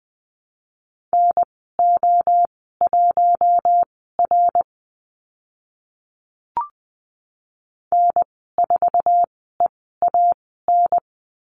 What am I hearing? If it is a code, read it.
Morse code, heard as a keyed tone, about 20 wpm, spaced as for 14 wpm, 700 Hz. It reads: NO1R N4EAN
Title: 062 Lesson 11g - Letter h - Call Signs 20wpm